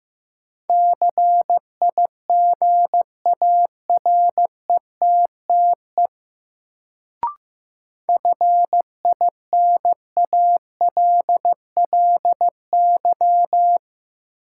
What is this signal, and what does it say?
Morse code, heard as a keyed tone, about 15 wpm, 700 Hz. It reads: CIGARETTE FINALLY
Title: Sentences from Top 1000 Words - New Word Drill - 15wpm